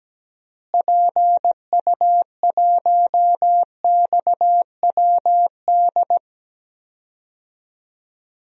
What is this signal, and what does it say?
Morse code, heard as a keyed tone, about 17 wpm, 700 Hz. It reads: PU1XWD
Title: DXCC-100 Call Signs - 6 Characters - Repeated 2x - 17wpm